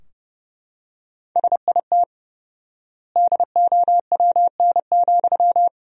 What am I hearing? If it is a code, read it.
Morse code, heard as a keyed tone, about 30 wpm, 700 Hz. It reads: SIT DOWN,
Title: 4x Word-Spacing - Sentences from Top 500 Words - 30wpm